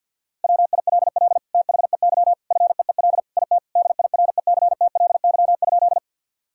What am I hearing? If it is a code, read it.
Morse code, heard as a keyed tone, about 50 wpm, 700 Hz. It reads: WILL THEY FEEL IT DIRECTLY?
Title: Sentences from Top 1000 Words - Review Part 5 - 50wpm